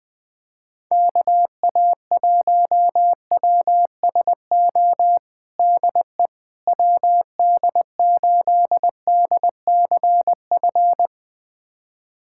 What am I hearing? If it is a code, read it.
Morse code, heard as a keyed tone, about 20 wpm, 700 Hz. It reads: KA1WSO DE WD8DCF
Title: USA Callsign DE Callsign - 6 Characters - 20wpm